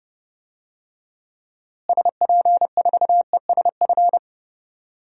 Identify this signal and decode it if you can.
Morse code, heard as a keyed tone, about 30 wpm, 700 Hz. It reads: SP4ESF
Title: DXCC-100 Call Signs - 6 Characters - Repeated 3x - 30wpm